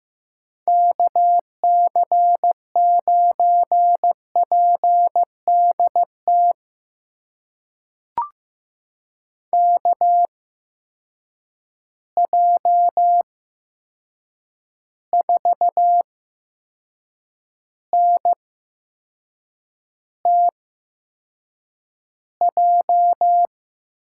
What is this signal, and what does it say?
Morse code, heard as a keyed tone, about 15 wpm, spaced as for 6 wpm, 700 Hz. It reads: KC9PDT KJ4NTJ
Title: USA Call Signs - 8x Character Spacing - 15wpm